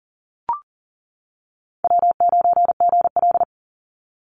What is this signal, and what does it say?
Morse code, heard as a keyed tone, about 40 wpm, 700 Hz. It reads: W9GL
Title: Easy Call Signs - 40wpm